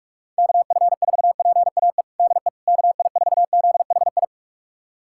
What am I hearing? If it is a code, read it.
Morse code, heard as a keyed tone, about 41 wpm, 690 Hz. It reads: KF4PAE DE KI4ZHI
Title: USA Callsign DE Callsign - 6 Characters - 45wpm